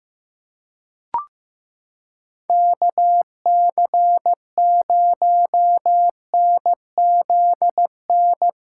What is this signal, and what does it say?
Morse code, heard as a keyed tone, about 15 wpm, 700 Hz. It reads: KC0NZN
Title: USA Callsign DE Callsign 15wpm